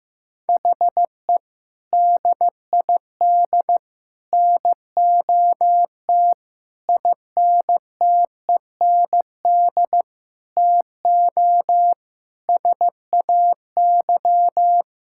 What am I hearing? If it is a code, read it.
Morse code, heard as a keyed tone, about 15 wpm, 700 Hz. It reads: HE DID NOT INTEND TO SAY
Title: Sentences from Top 1600 Words 15wpm